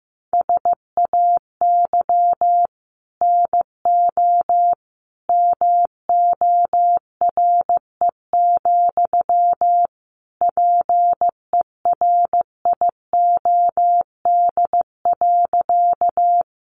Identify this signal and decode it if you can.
Morse code, heard as a keyed tone, about 15 wpm, 700 Hz. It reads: SAY NO MORE, PERIOD.
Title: Sentences from Top 800 Words 15wpm